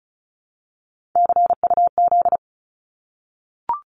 Morse code, heard as a keyed tone, about 34 wpm, 700 Hz. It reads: CUZ
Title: Abbreviations Repeated 3x - 35wpm